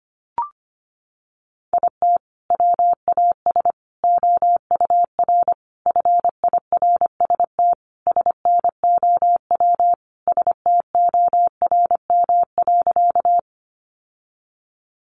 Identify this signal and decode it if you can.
Morse code, heard as a keyed tone, about 25 wpm, 700 Hz. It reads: IT WAS OUR FIRST SNOW STORM.
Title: Sentences from Top 1500 Words 25wpm